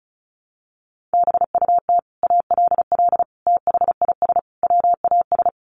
Morse code, heard as a keyed tone, about 34 wpm, 700 Hz. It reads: BUT ALL THIS WAS
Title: Sentences from Top 1400 Words 35wpm